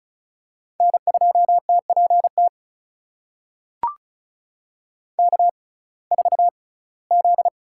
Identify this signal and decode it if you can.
Morse code, heard as a keyed tone, about 34 wpm, spaced as for 21 wpm, 700 Hz. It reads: N2TPT K4Z
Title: USA Call Signs - 5 Characters - 6x Character Spacing - 35wpm